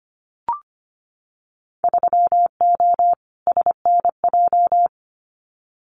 Morse code, heard as a keyed tone, about 25 wpm, 700 Hz. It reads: 3O SNJ
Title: Winter Field Day Exchanges Encore 25wpm